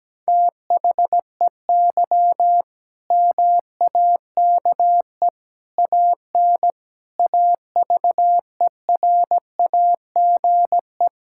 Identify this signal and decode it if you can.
Morse code, heard as a keyed tone, about 17 wpm, 700 Hz. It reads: THEY MAKE AN AVERAGE
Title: Sentences from Top 1000 Words - Review Part 6 - 17wpm